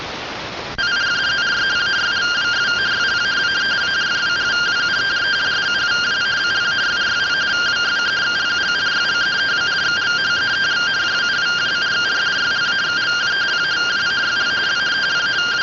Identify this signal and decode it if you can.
Two-tone teleprinter signal, about 170 Hz shift.
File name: findings / 202 coronal iodine 3mm.gen-43 rtty rx